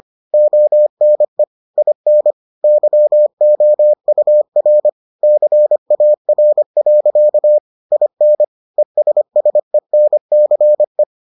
Morse code, heard as a keyed tone, about 25 wpm, 600 Hz. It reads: ONE IN YOUR CAR. IN ESSENCE